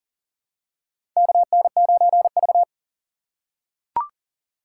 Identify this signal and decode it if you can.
Morse code, heard as a keyed tone, about 40 wpm, 700 Hz. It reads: KN9V
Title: USA Call Signs - 4 Characters - 4x Character Spacing - 40wpm